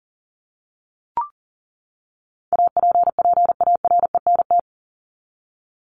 Morse code, heard as a keyed tone, about 40 wpm, 700 Hz. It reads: APPARENT
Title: LY Suffix - Top 100 Words with Base Words 40wpm